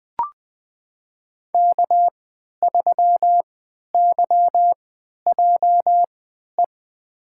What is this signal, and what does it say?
Morse code, heard as a keyed tone, about 20 wpm, spaced as for 11 wpm, 700 Hz. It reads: K3YJE
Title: USA Call Signs - 5 Characters - 3x Character Spacing - 20wpm